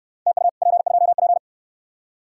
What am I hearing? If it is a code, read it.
Morse code, heard as a keyed tone, about 64 wpm, 700 Hz. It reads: TU P2P